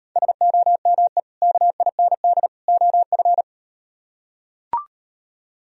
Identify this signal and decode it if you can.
Morse code, heard as a keyed tone, about 38 wpm, 700 Hz. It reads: SOME KIND OF